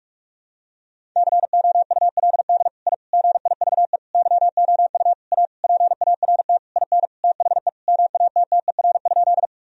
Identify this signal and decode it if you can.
Morse code, heard as a keyed tone, about 44 wpm, 700 Hz. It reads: COULD I GIVE YOU A PART IN THE MATTER?
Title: Sentences from Top 1000 Words - Review Part 1 - 45wpm